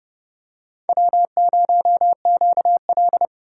Morse code, heard as a keyed tone, about 28 wpm, 700 Hz. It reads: W0QL